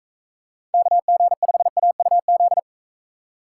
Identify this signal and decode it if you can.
Morse code, heard as a keyed tone, about 41 wpm, 700 Hz. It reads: KG5AUZ